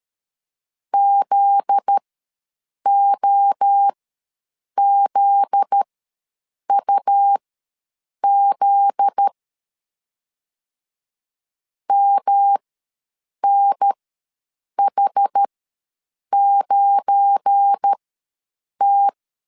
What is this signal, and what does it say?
Morse code, heard as a keyed tone, about 13 wpm, spaced as for 7 wpm, 800 Hz. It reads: ZOZUZ MNH9T
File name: 7 GF9Z ws 12wpm